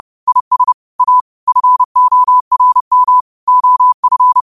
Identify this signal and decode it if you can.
Morse code, heard as a keyed tone, about 30 wpm, 1000 Hz. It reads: IS A FORM OF
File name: qotdfiles / 30wpm